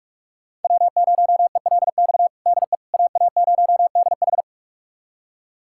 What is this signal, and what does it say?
Morse code, heard as a keyed tone, about 46 wpm, 700 Hz. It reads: W0ELX DE AA0DH
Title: USA Callsign DE Callsign - 5 Characters - 45wpm